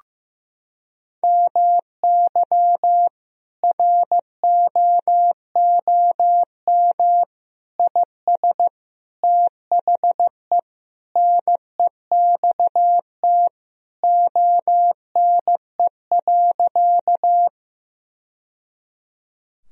Morse code, heard as a keyed tone, about 16 wpm, 700 Hz. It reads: MY ROOM IS THE NEXT ONE.
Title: Sentences from Top 1000 Words - Review Part 1 - 15wpm